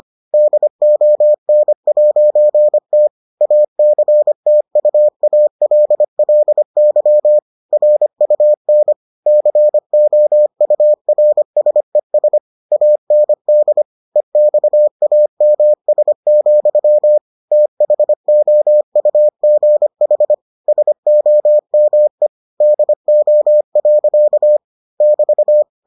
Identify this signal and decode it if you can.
Morse code, heard as a keyed tone, about 25 wpm, 600 Hz. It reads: DON'T ACTUALLY RUN COURSES AND EXAMS, THOUGH SOME DO. <BT>